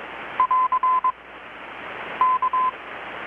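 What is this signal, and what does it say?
Morse code, heard as a keyed tone, about 22 wpm, 1020 Hz. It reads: <AR> K
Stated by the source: Morse from KPH's Night of Nights XX maritime radiotelegraphy event, received in Chiapas, Mexico about 3,800 km away on a Kenwood R-600 with a grounded 10 m long-wire antenna.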